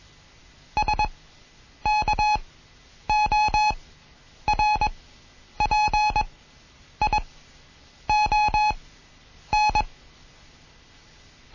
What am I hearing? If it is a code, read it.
Morse code, heard as a keyed tone, about 21 wpm, spaced as for 9 wpm, 850 Hz. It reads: SKORPION
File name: Morsen / Kurs A 20